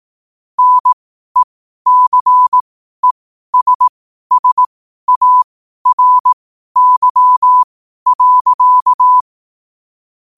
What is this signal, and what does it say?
Morse code, heard as a keyed tone, about 18 wpm, spaced as for 12 wpm, 1000 Hz. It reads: NECESSARY.
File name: qotdfiles / 12wpm